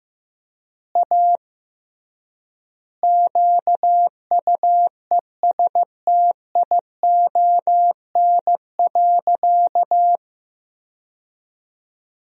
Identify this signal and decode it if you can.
Morse code, heard as a keyed tone, about 15 wpm, 700 Hz. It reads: A QUESTION.
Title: 3x Word-Spacing - Sentences from Top 400 Words - 15wpm